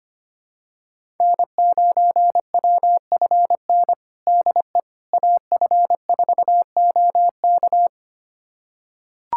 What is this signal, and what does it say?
Morse code, heard as a keyed tone, about 25 wpm, 700 Hz. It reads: N9WFN DE AF4OK
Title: USA Callsign DE Callsign - 5 Characters - 25wpm